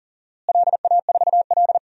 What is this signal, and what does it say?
Morse code, heard as a keyed tone, about 40 wpm, 700 Hz. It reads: LA4L
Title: DXCC-100 Call Signs - 4 Characters - Repeated 3x - 40wpm